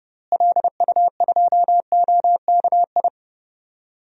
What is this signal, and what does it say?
Morse code, heard as a keyed tone, about 31 wpm, 700 Hz. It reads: LU2OKI
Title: DXCC-100 Call Signs - 6 Characters - Repeated 3x - 30wpm